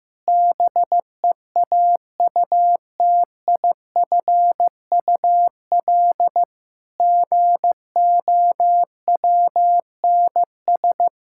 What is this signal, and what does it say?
Morse code, heard as a keyed tone, about 15 wpm, 700 Hz. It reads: BEAUTIFUL GOWNS
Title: Pangram Challenge v2 - 15wpm